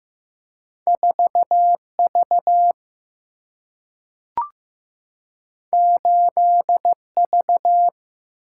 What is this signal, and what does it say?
Morse code, heard as a keyed tone, about 15 wpm, 700 Hz. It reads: 4V 8V
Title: Single-Letter Single-Number 15wpm